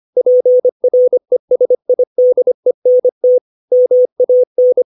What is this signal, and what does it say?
Morse code, heard as a keyed tone, about 25 wpm, 500 Hz. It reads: PRESIDENT MAN
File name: 632 words / 632 Words 25WPM